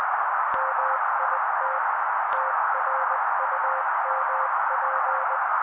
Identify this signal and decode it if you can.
Morse code, heard as a keyed tone, about 20 wpm, 530 Hz, with noise. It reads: MIT TRUMP
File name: daily / 18wpm noise